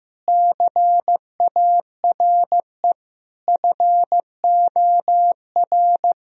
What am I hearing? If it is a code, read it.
Morse code, heard as a keyed tone, about 15 wpm, 700 Hz. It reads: CARE FOR
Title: Sentences from Top 1000 Words - Review Part 1 - 15wpm